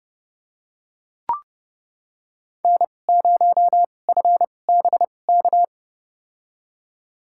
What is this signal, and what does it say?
Morse code, heard as a keyed tone, about 30 wpm, spaced as for 22 wpm, 690 Hz. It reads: N0FBK
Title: USA Call Signs - 5 Characters - 2x Character Spacing - 30wpm